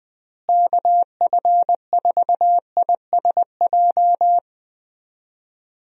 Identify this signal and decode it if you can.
Morse code, heard as a keyed tone, about 21 wpm, 700 Hz. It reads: KF4ISJ
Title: USA Call Signs - 6 Characters - Repeated 3x - 20wpm